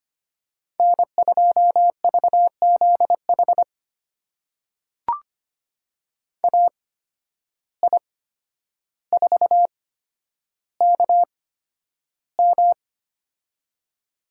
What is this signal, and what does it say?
Morse code, heard as a keyed tone, about 25 wpm, spaced as for 11 wpm, 700 Hz. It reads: N2VZH AI4KM
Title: USA Call Signs - 5 Characters - 8x Character Spacing - 25wpm